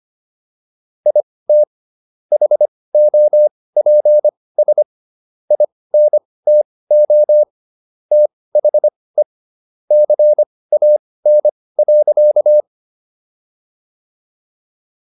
Morse code, heard as a keyed tone, about 25 wpm, spaced as for 18 wpm, 600 Hz. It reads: IT HOPS INTO THE CAN.